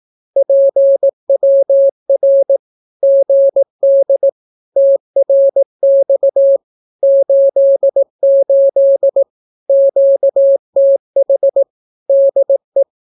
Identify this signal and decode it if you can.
Morse code, heard as a keyed tone, about 18 wpm, 550 Hz. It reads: PWR GD TRX 88 QTH DE
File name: naukapodstaw / skroty / skroty